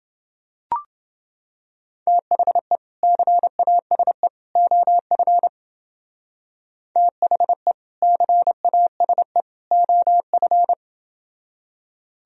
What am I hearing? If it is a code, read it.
Morse code, heard as a keyed tone, about 31 wpm, 700 Hz. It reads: THE CASE OF THE CASE OF